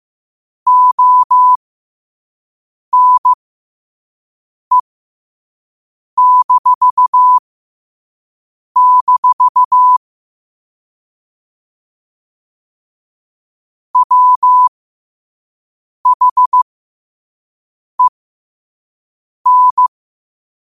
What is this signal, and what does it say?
Morse code, heard as a keyed tone, about 15 wpm, spaced as for 5 wpm, 1000 Hz. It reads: ONE-- WHEN
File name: qotdfiles / 5wpm